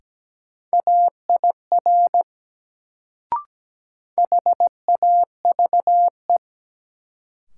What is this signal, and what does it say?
Morse code, heard as a keyed tone, about 17 wpm, 700 Hz. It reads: AIR HAVE
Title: Sets of 2 Words - New Word Drill - 17wpm